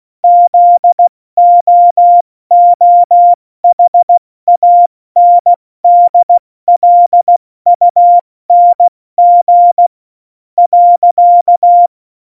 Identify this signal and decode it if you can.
Morse code, heard as a keyed tone, about 16 wpm, 700 Hz. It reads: ZOOHANDLUNG .
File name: Morsekurs / Witze / Witz 23. Oktober 2018, 70 F80